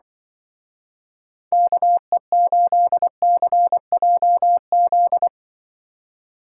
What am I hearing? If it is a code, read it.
Morse code, heard as a keyed tone, about 24 wpm, 700 Hz. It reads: KE8CJZ
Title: Speed Racing - USA 2x3 Call Signs 20wpm